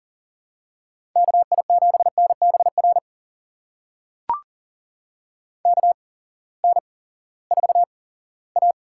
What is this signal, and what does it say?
Morse code, heard as a keyed tone, about 40 wpm, spaced as for 20 wpm, 700 Hz. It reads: KI7NBR KN4A
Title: USA Call Signs - 6 Characters - 8x Character Spacing - 40wpm